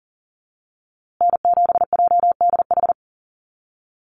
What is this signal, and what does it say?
Morse code, heard as a keyed tone, about 40 wpm, 700 Hz. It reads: N7JDH
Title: USA Call Signs Repeated 2x 40wpm